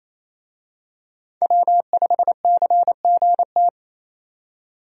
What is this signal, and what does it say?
Morse code, heard as a keyed tone, about 28 wpm, 700 Hz. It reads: W5CGT
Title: USA Call Signs - 5 Characters - 28wpm